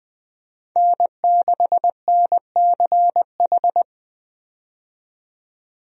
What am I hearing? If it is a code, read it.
Morse code, heard as a keyed tone, about 20 wpm, 700 Hz. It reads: N6NCH
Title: USA Call Signs - 5 Characters - 20wpm